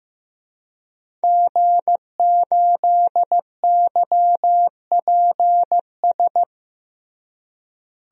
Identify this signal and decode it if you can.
Morse code, heard as a keyed tone, about 15 wpm, 700 Hz. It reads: G8YPS
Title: DXCC-100 Call Signs - 5 Characters - Repeated 2x - 15wpm